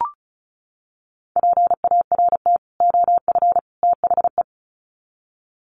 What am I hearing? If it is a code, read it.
Morse code, heard as a keyed tone, about 34 wpm, 700 Hz. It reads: PART OF THE
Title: Sets of 3 Words 35wpm